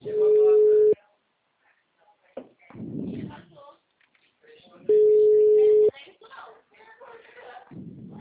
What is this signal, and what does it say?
Ringback tone (Europe, 425 Hz), with speech.